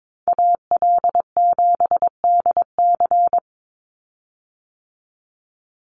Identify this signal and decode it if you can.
Morse code, heard as a keyed tone, about 22 wpm, 700 Hz. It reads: AL7DC
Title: USA Call Signs 22wpm